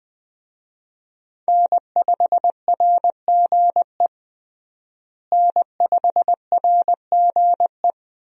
Morse code, heard as a keyed tone, about 20 wpm, 700 Hz. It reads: N5RGE N5RGE
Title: USA Call Signs - 5 Characters - Repeated 2x - 20wpm